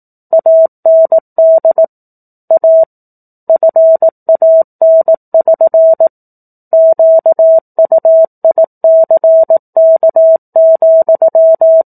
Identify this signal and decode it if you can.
Morse code, heard as a keyed tone, about 18 wpm, 650 Hz. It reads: AND A FAN<SN> QUICK,
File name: 04-Chapter-04 18 WPM